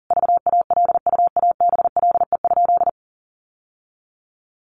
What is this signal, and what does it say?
Morse code, heard as a keyed tone, about 40 wpm, 700 Hz. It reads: VALUABLE?
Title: Sentences from Top 1100 Words 40wpm